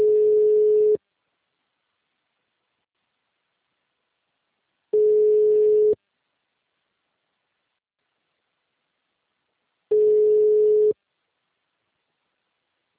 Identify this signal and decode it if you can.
Ringback tone (Europe, 425 Hz).
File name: out- -2046